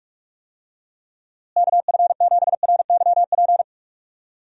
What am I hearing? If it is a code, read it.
Morse code, heard as a keyed tone, about 48 wpm, 700 Hz. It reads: KF7RYP
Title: USA Call Signs - 6 Characters - Repeated 3x - 45wpm